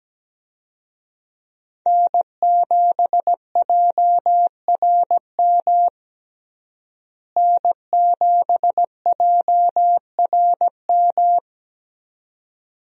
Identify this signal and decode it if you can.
Morse code, heard as a keyed tone, about 17 wpm, 700 Hz. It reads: N7JRM N7JRM
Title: USA Call Signs - 5 Characters - Repeated 2x - 17wpm